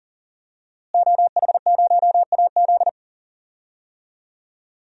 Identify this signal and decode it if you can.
Morse code, heard as a keyed tone, about 40 wpm, 700 Hz. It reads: OH0AZ
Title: DXCC-100 Call Signs - 5 Characters - Repeated 3x - 40wpm